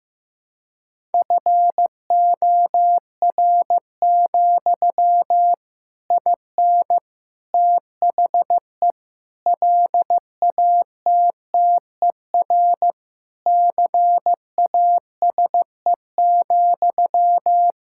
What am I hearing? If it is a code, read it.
Morse code, heard as a keyed tone, about 15 wpm, 700 Hz. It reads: FOR, IN THE LATTER CASE,
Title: Sentences from Top 1600 Words 15wpm